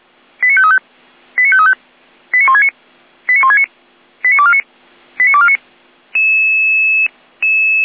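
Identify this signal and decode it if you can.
Five-tone selective call (ZVEI), heard as quick runs of five tones, tones of about 70 ms. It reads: 89535 89535 89169 89169 89249 89249
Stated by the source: ZVEI five-tone selective call received on an SDR.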